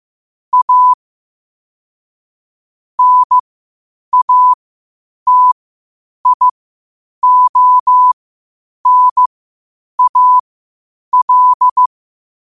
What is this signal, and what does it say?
Morse code, heard as a keyed tone, about 15 wpm, spaced as for 8 wpm, 1000 Hz. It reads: A NATIONAL
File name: qotdfiles / 8wpm